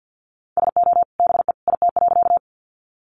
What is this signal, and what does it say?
Morse code, heard as a keyed tone, about 50 wpm, 700 Hz. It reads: SO BE IT.
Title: Sentences from Top 100 Words 50wpm